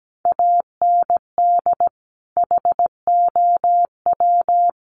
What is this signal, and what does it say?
Morse code, heard as a keyed tone, about 17 wpm, 700 Hz. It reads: AND HOW
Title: Sentences from Top 200 Words 17wpm